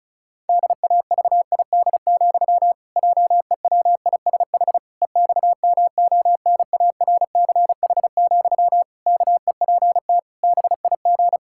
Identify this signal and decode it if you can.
Morse code, heard as a keyed tone, about 34 wpm, 700 Hz. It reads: DAVID, JEWISH EXMONARCH, KEPT BIG
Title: Pangram Challenge v2 - 35wpm